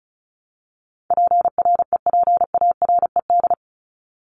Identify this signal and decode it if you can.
Morse code, heard as a keyed tone, about 34 wpm, 700 Hz. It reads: PREPARED
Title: ED Suffix - Top 100 Words with Base Words 35wpm